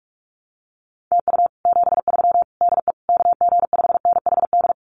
Morse code, heard as a keyed tone, about 46 wpm, 700 Hz. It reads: TU 73 DE KG5NHD
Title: Common QSO Phrases Encore 45wpm